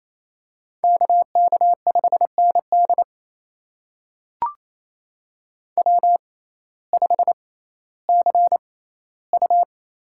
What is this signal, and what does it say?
Morse code, heard as a keyed tone, about 28 wpm, spaced as for 15 wpm, 700 Hz. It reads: KK5ND W5CU
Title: USA Call Signs - 5 Characters - 6x Character Spacing - 28wpm